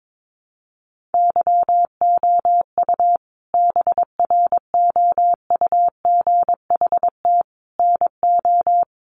Morse code, heard as a keyed tone, about 22 wpm, 700 Hz. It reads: YOU BROUGHT NO
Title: Sentences from Top 500 Words 22wpm